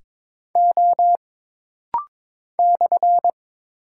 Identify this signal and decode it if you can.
Morse code, heard as a keyed tone, about 22 wpm, 700 Hz. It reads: O /
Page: This rapid-fire timing is geared to push your ICR (Instant Character Recognition) skills.